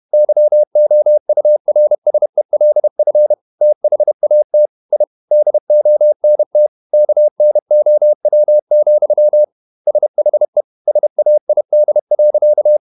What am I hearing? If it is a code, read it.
Morse code, heard as a keyed tone, about 31 wpm, 600 Hz. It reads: YOURSELF THAT I DONT KNOW, SHE SAID.
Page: Never 31wpm | CW med Gnister